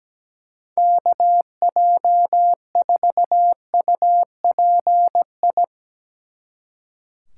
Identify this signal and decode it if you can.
Morse code, heard as a keyed tone, about 17 wpm, 700 Hz. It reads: KJ4UPI
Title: Speed Racing - USA 2x3 Call Signs 17wpm